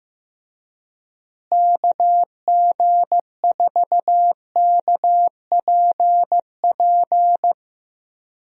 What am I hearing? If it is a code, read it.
Morse code, heard as a keyed tone, about 15 wpm, 700 Hz. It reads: KG4KPP
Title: USA Call Signs - 6 Characters - Repeated 3x - 15wpm